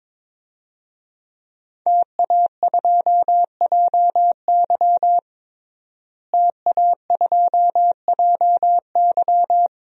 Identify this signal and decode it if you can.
Morse code, heard as a keyed tone, about 21 wpm, 700 Hz. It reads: TA2JY TA2JY
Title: DXCC-100 Call Signs - 5 Characters - Repeated 2x - 22wpm